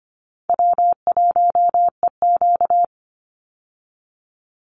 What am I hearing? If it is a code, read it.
Morse code, heard as a keyed tone, about 25 wpm, 700 Hz. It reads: W1EQ
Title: Easy Call Signs Repeated 2x - 25wpm